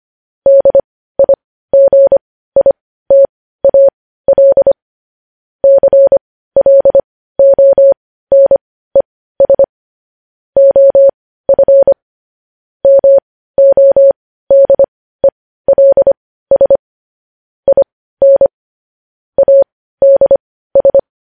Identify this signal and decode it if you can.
Morse code, heard as a keyed tone, about 25 wpm, spaced as for 15 wpm, 550 Hz. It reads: DIGITAL CLONES OF MODELS IN ADS
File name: podcast / News headlines-1x-25x15